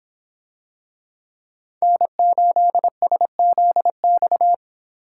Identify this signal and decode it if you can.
Morse code, heard as a keyed tone, about 25 wpm, 700 Hz. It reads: N8SZX